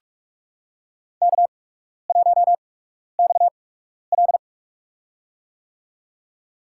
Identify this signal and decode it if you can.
Morse code, heard as a keyed tone, about 46 wpm, spaced as for 12 wpm, 690 Hz. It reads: K1XL
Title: USA Call Signs - 4 Characters - 8x Character Spacing - 45wpm